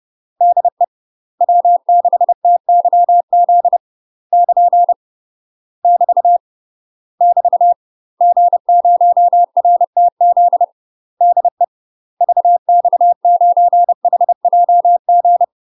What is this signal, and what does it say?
Morse code, heard as a keyed tone, about 30 wpm, 700 Hz. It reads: DE W6TYZ <KN> <BT> <BT> G0RTZ DE VX9HJG
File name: UK MASTER QSOs TRAINING / 30WPM / UK MASTER QSOs 0046